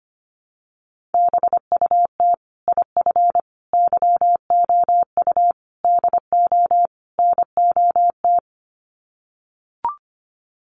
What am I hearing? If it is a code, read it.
Morse code, heard as a keyed tone, about 25 wpm, 700 Hz. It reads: BUT IF YOU DO NOT
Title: Sets of 5 Words Encore 25wpm